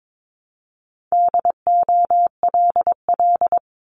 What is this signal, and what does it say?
Morse code, heard as a keyed tone, about 22 wpm, 700 Hz. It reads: DOLL
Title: Multi-letter Phonemes Repeated 3x 22wpm